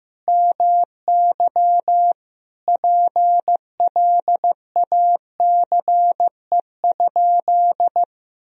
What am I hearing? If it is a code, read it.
Morse code, heard as a keyed tone, about 15 wpm, 700 Hz. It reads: MY PLACE?
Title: Sentences from Top 200 Words and Abbreviations 15wpm